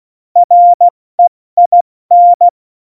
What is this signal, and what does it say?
Morse code, heard as a keyed tone, about 16 wpm, 700 Hz. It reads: REIN
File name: Morsekurs / Witze / Witz 10. September 2019, 70 F80